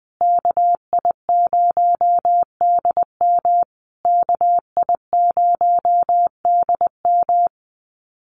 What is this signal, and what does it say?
Morse code, heard as a keyed tone, about 20 wpm, 700 Hz. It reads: KI0DM KI0DM
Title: USA Call Signs Repeated 3x 20wpm